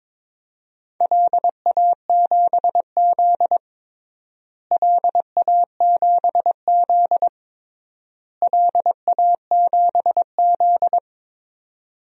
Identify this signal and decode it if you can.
Morse code, heard as a keyed tone, about 21 wpm, 700 Hz. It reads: LA7Z LA7Z LA7Z
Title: DXCC-100 Call Signs - 4 Characters - Repeated 3x - 22wpm